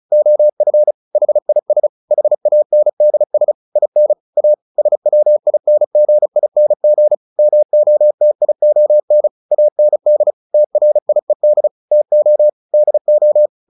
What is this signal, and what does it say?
Morse code, heard as a keyed tone, about 34 wpm, 600 Hz. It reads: OF HIS HANDS IN A SWINGING MOTION AND TRIED TO DO
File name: the old man and the sea 2 35wpm